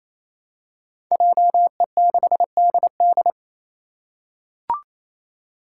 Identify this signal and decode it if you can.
Morse code, heard as a keyed tone, about 27 wpm, 700 Hz. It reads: JE6DD
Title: Five Challenging Letter-Number Combinations 28wpm